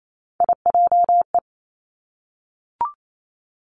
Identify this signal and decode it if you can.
Morse code, heard as a keyed tone, about 28 wpm, 700 Hz. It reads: IJE
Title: Three Challenging Letter-Number Combinations 28wpm